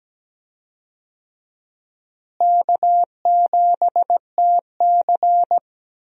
Morse code, heard as a keyed tone, about 17 wpm, 700 Hz. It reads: K7TC
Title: USA Call Signs - 4 Characters - Repeated 2x - 17wpm